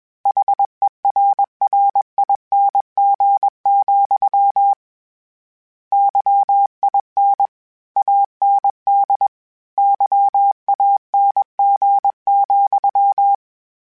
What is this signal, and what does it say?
Morse code, heard as a keyed tone, about 21 wpm, 800 Hz. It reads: HERRING, YIN AND YANG,
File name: Misc / Misc-mp3s / Word Pairs @20